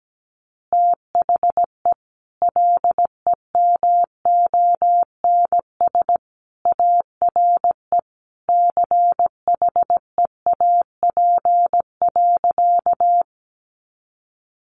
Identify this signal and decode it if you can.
Morse code, heard as a keyed tone, about 17 wpm, 700 Hz. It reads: THE LEMONS ARE CHEAP.
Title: Additional Easy Phrases 17wpm